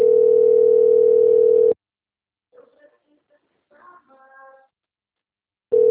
Ringback tone (North America), 440 + 480 Hz.
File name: out- - - - - .2541